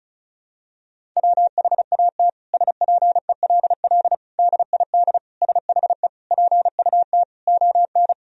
Morse code, heard as a keyed tone, about 36 wpm, 700 Hz. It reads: WHAT SPELL DID SHE PUT ON
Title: Sentences from Top 1000 Words - Review Part 1 - 35wpm